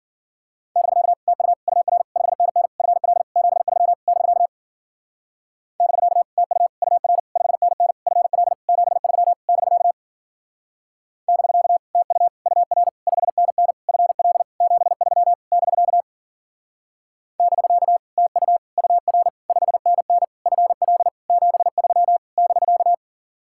Morse code, heard as a keyed tone, about 52 wpm, 700 Hz. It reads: <BK> TU UR 5NN FL 73 <BK> <BK> TU UR 5NN FL 73 <BK> <BK> TU UR 5NN FL 73 <BK> <BK> TU UR 5NN FL 73 <BK>